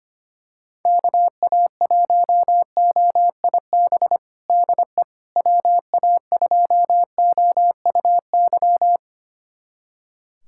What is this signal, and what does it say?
Morse code, heard as a keyed tone, about 25 wpm, 700 Hz. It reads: KA1OIB DE WA2OUY
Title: USA Callsign DE Callsign 25wpm